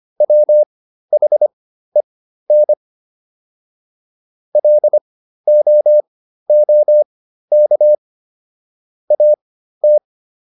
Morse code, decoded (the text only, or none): WHEN LOOK AT